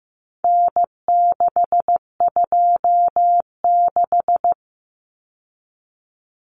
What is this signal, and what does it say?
Morse code, heard as a keyed tone, about 15 wpm, 700 Hz. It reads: N626
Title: Common Cut Numbers - 4-Digit 15wpm